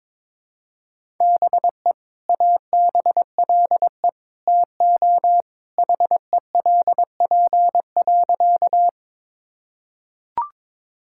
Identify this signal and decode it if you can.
Morse code, heard as a keyed tone, about 21 wpm, 700 Hz. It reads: BE ABLE TO HELP.
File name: practice / 22wpm / Sentences with 2-Word Phrases - 22wpm